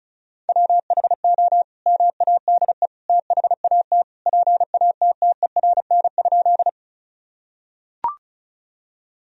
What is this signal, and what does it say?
Morse code, heard as a keyed tone, about 34 wpm, 700 Hz. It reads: WHO MADE THAT PATTERN?
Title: Sentences from Top 1000 Words - Review Part 2 - 35wpm